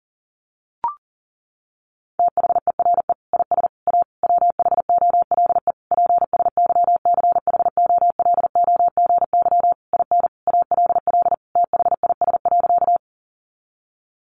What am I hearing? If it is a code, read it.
Morse code, heard as a keyed tone, about 37 wpm, 700 Hz. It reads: THERE IS A WHOLE PSYCHOLOGY IN ALL THIS.
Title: Sentences from Top 1400 Words 40wpm